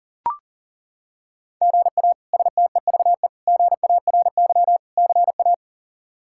Morse code, heard as a keyed tone, about 40 wpm, 700 Hz. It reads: GA STEVE GARY CA
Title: K1USN SST QSO Practice - 40wpm